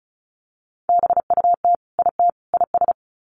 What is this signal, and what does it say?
Morse code, heard as a keyed tone, about 34 wpm, 700 Hz. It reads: BUT IT IS
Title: Sets of 3 Words 35wpm